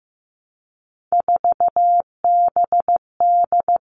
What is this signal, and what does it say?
Morse code, heard as a keyed tone, about 15 wpm, 700 Hz. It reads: 4BD
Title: Three Challenging Letter-Number Combinations 15wpm